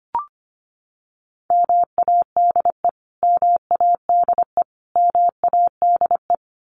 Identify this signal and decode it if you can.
Morse code, heard as a keyed tone, about 25 wpm, 700 Hz. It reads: MADE MADE MADE
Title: Top 200 Words Repeated 25wpm